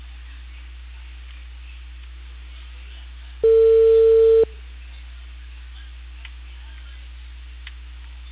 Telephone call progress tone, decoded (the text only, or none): ringback tone